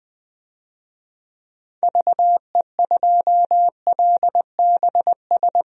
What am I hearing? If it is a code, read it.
Morse code, heard as a keyed tone, about 20 wpm, 700 Hz. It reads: VE2LBS
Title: DXCC-100 Call Signs - 6 Characters - Repeated 2x - 20wpm